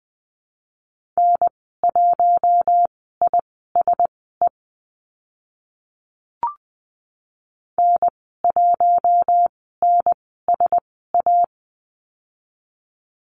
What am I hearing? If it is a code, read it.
Morse code, heard as a keyed tone, about 20 wpm, spaced as for 14 wpm, 700 Hz. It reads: N1ISE N1NSA
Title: 042 Lesson 8g - Number 1 - Call Signs 20wpm